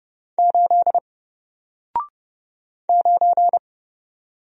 Morse code, heard as a keyed tone, about 30 wpm, 700 Hz. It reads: 8 9
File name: Single-Letter-Number-30wpm